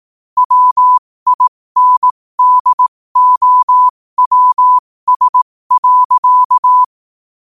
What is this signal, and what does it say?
Morse code, heard as a keyed tone, about 18 wpm, spaced as for 15 wpm, 1000 Hz. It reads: WINDOWS.
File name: qotdfiles / 15wpm